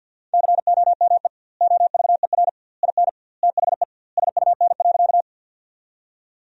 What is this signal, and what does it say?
Morse code, heard as a keyed tone, about 52 wpm, 700 Hz. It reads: COME OVER IN THE SUN.
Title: 2x Word-Spacing - Sentences from Top 300 Words - 50wpm